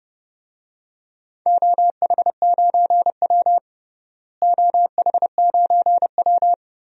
Morse code, heard as a keyed tone, about 30 wpm, 700 Hz. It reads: OH9W OH9W
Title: DXCC-100 Call Signs - 4 Characters - Repeated 3x - 30wpm